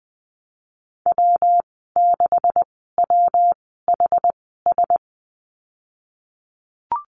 Morse code, heard as a keyed tone, about 20 wpm, spaced as for 15 wpm, 700 Hz. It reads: W6WHS
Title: 245 Lesson 39g - Letter z - Call Signs 20wpm